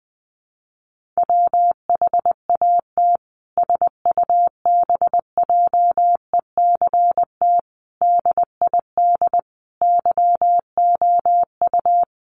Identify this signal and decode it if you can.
Morse code, heard as a keyed tone, about 20 wpm, 700 Hz. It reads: WHAT SUBJECT DID YOU
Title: Sentences from Top 1500 Words 20wpm